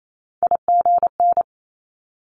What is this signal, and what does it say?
Morse code, heard as a keyed tone, about 28 wpm, 700 Hz. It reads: IGN
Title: Multi-letter Phonemes Quick Fire 28wpm